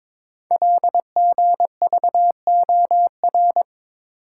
Morse code, heard as a keyed tone, about 23 wpm, 700 Hz. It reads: LGVOR
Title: Code Groups - 5 Letters - 22wpm